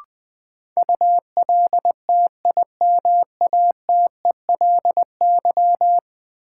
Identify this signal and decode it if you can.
Morse code, heard as a keyed tone, about 20 wpm, 700 Hz. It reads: ULTIMATELY
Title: Sentences from Top 1200 Words - New Word Drill Alt - 20wpm